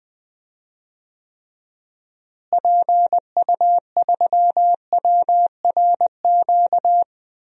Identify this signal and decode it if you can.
Morse code, heard as a keyed tone, about 20 wpm, 700 Hz. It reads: PU3WRQ
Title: DXCC-100 Call Signs - 6 Characters - Repeated 2x - 20wpm